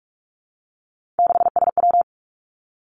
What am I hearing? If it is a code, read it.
Morse code, heard as a keyed tone, about 46 wpm, 700 Hz. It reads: 6SW
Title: Three Challenging Letter-Number Combinations 45wpm